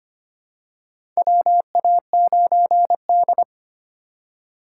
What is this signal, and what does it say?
Morse code, heard as a keyed tone, about 25 wpm, 700 Hz. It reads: WA9D
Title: USA Call Signs - 3x Character Spacing - 25wpm